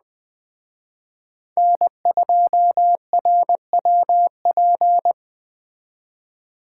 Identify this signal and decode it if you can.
Morse code, heard as a keyed tone, about 20 wpm, 700 Hz. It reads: N2RWP
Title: Speed Racing - USA 1x3 Call Signs 17wpm